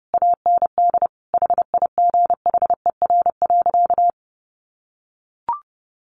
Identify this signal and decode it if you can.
Morse code, heard as a keyed tone, about 30 wpm, 700 Hz. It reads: AND HIGHER.
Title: Additional Easy Phrases 30wpm